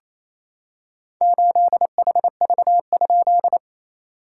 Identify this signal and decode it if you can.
Morse code, heard as a keyed tone, about 28 wpm, 660 Hz. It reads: 8HV?
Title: SOTA QSO Practice 28wpm